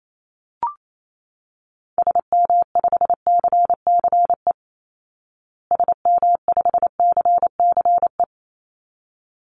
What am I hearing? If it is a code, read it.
Morse code, heard as a keyed tone, about 28 wpm, 700 Hz. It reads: SM5CCE SM5CCE
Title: Call Signs Repeated 3x - 28wpm